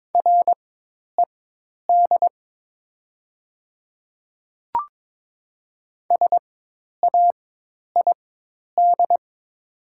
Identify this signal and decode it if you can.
Morse code, heard as a keyed tone, about 22 wpm, spaced as for 10 wpm, 700 Hz. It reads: RED SAID
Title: 4x Character Spacing - Top 500 Words - 22wpm